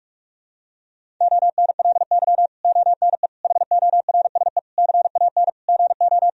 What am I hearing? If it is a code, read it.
Morse code, heard as a keyed tone, about 46 wpm, 700 Hz. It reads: ONLY ONE HORSE CAN GO
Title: Sentences from Top 1000 Words - Review Part 1 - 45wpm